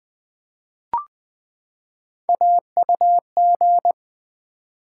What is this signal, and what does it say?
Morse code, heard as a keyed tone, about 20 wpm, 700 Hz. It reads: AUG
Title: Build A Word - 3 Letters At A Time - 6-Letter Words - 20wpm